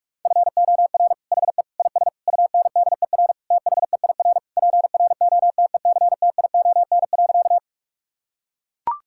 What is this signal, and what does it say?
Morse code, heard as a keyed tone, about 41 wpm, 700 Hz. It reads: FOR HE IS UNDER THEIR PROTECTION.
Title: Sentences from Top 1600 Words 45wpm